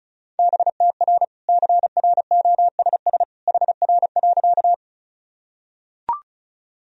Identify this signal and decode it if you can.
Morse code, heard as a keyed tone, about 34 wpm, 700 Hz. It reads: BTR CROSS HR.
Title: Sentences from Top 400 Words and Abbreviations 35wpm